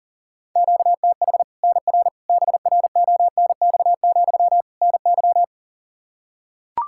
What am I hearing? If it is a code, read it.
Morse code, heard as a keyed tone, about 40 wpm, 700 Hz. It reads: QTH NR BRONX, NY
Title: Practice QSO - Exchange 1 - Beginning - 40wpm